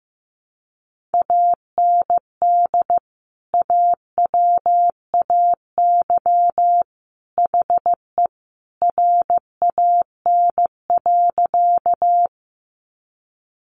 Morse code, heard as a keyed tone, about 15 wpm, 700 Hz. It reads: AND AWAY HE RAN.
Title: Sentences from Top 600 Words 15wpm